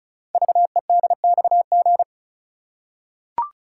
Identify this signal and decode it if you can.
Morse code, heard as a keyed tone, about 36 wpm, 700 Hz. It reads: VEDXG
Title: Code Groups - 5 Letters - 35wpm